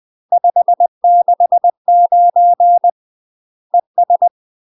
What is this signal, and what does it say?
Morse code, heard as a keyed tone, about 20 wpm, 700 Hz. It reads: 569 ES
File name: UK MASTER QSOs TRAINING / 20WPM / UK MASTER QSOs 0034